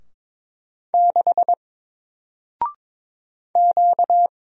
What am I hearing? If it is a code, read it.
Morse code, heard as a keyed tone, about 22 wpm, 700 Hz. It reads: 6 Q
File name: Single-Letter-Number-22wpm